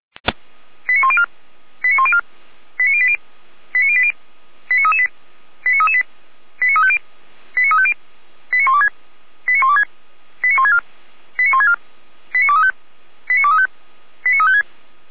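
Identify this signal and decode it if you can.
Five-tone selective call (ZVEI), heard as quick runs of five tones, tones of about 70 ms. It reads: 89104 89104 89080 89080 89338 89338 89350 89350 89126 89126 89164 89164 89235 89235 89356
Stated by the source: ZVEI five-tone selective call received on SDR.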